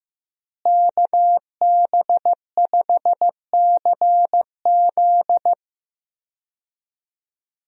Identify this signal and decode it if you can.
Morse code, heard as a keyed tone, about 16 wpm, 700 Hz. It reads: KB5CZ
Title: USA Call Signs - 5 Characters - Repeated 2x - 15wpm